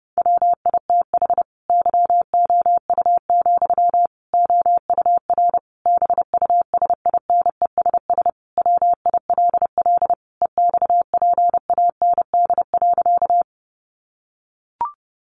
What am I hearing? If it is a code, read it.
Morse code, heard as a keyed tone, about 28 wpm, 700 Hz. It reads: WITH YOU, OUR BUSINESS WILL EXPAND.
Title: Sentences from Top 1400 Words 30wpm